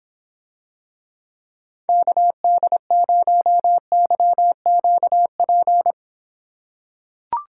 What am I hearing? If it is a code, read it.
Morse code, heard as a keyed tone, about 26 wpm, 700 Hz. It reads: KD0YQP